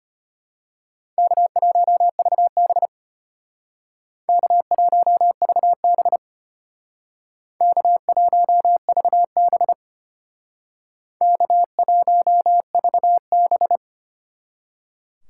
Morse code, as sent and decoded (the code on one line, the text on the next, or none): -.- .---- ...- -... / -.- .---- ...- -... / -.- .---- ...- -... / -.- .---- ...- -...
K1VB K1VB K1VB K1VB